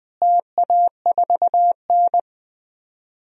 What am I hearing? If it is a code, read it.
Morse code, heard as a keyed tone, about 20 wpm, 700 Hz. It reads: TA4N
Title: DXCC-100 Call Signs - 4 Characters - Repeated 3x - 20wpm